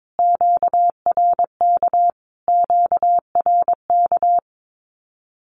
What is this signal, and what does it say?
Morse code, heard as a keyed tone, about 22 wpm, 700 Hz. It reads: QRK QRK
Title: Q-Codes Repeated 3x 22wpm